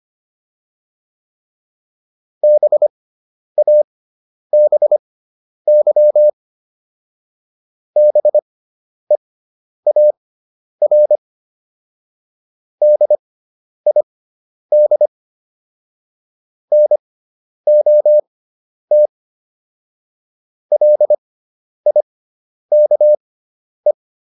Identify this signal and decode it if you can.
Morse code, heard as a keyed tone, about 25 wpm, spaced as for 10 wpm, 600 Hz. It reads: BABY BEAR DID NOT LIKE